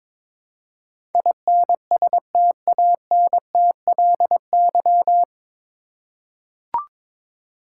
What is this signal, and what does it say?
Morse code, heard as a keyed tone, about 22 wpm, 700 Hz. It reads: INSTANTLY
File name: 9 Letter Words - First 3 Letters Given - 22wpm